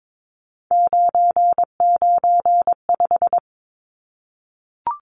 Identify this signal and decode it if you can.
Morse code, heard as a keyed tone, about 22 wpm, 700 Hz. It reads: 995
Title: Numbers - 3-Digit 22wpm